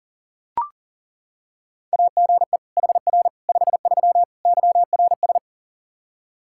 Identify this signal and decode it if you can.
Morse code, heard as a keyed tone, about 40 wpm, 700 Hz. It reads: AGE HR 53 YRS
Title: Practice QSO - Exchange 3 - More Info - 40wpm